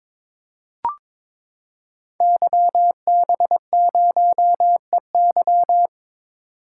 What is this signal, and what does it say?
Morse code, heard as a keyed tone, about 22 wpm, 700 Hz. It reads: YB0EY
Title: Code Groups - 5 Letter:Numbers - 22wpm